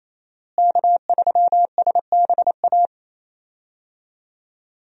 Morse code, heard as a keyed tone, about 28 wpm, 700 Hz. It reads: K3SBA
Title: USA Call Signs - 5 Characters - Repeated 2x - 28wpm